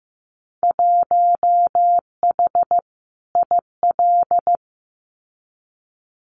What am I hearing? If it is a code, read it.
Morse code, heard as a keyed tone, about 15 wpm, 700 Hz. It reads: 1H IL
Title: Winter Field Day Exchanges 15wpm